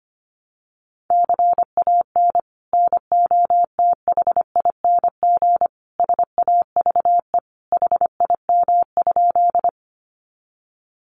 Morse code, heard as a keyed tone, about 25 wpm, 700 Hz. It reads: CAN NOTHING SAVE HIM?
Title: Sentences from Top 1100 Words 25wpm